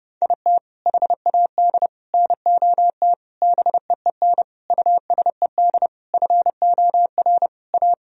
Morse code, heard as a keyed tone, about 30 wpm, 700 Hz. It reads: IT HAD NOT BEEN USED FOR A
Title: Sentences from Top 1000 Words - Review Part 1 - 30wpm